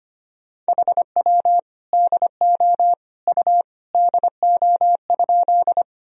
Morse code, heard as a keyed tone, about 25 wpm, 700 Hz. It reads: HW DO U DO?
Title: Sentences from Top 100 Words and Abbreviations 25wpm